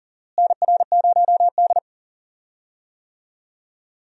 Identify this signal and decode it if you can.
Morse code, heard as a keyed tone, about 40 wpm, 700 Hz. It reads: NR0D
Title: USA Call Signs - 4 Characters - Repeated 3x - 40wpm